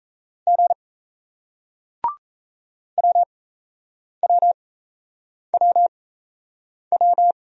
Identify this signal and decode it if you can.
Morse code, heard as a keyed tone, about 35 wpm, spaced as for 8 wpm, 700 Hz. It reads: G WWWW
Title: Speed-Racing Single Letter-Number 28wpm